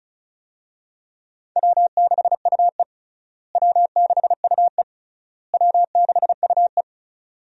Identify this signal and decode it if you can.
Morse code, heard as a keyed tone, about 34 wpm, 700 Hz. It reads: W6UE W6UE W6UE
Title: USA Call Signs - 4 Characters - Repeated 3x - 35wpm